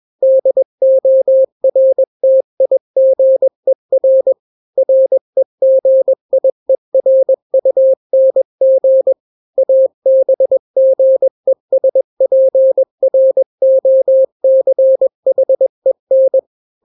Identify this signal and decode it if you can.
Morse code, heard as a keyed tone, about 21 wpm, 530 Hz. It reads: DORTIGER REGIERUNG ABGESPROCHEN
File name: daily / 20wpm